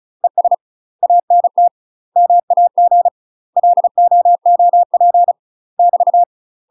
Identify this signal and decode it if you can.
Morse code, heard as a keyed tone, about 34 wpm, 700 Hz. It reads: ES ANT MAG LOOP <BT>
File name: UK MASTER QSOs TRAINING/35WPM